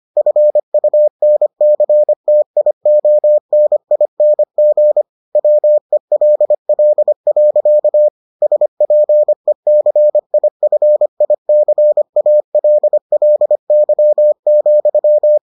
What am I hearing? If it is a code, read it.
Morse code, heard as a keyed tone, about 25 wpm, 600 Hz. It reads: FUNCTIONING WELL. SPECIFICALLY,